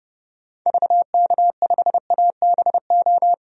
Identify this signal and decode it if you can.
Morse code, heard as a keyed tone, about 30 wpm, 700 Hz. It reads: VK5ABO
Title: DXCC-100 Call Signs - 6 Characters - 30wpm